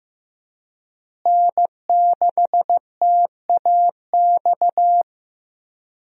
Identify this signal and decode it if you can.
Morse code, heard as a keyed tone, about 15 wpm, 700 Hz. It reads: N6TAX
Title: USA Call Signs - 5 Characters - Repeated 3x - 15wpm